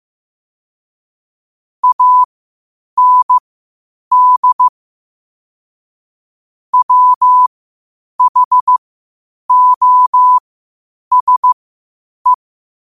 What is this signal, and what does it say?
Morse code, heard as a keyed tone, about 15 wpm, spaced as for 8 wpm, 1000 Hz. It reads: AND WHOSE
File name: qotdfiles / 8wpm